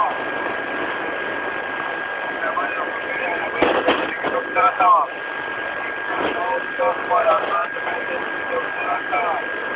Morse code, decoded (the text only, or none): none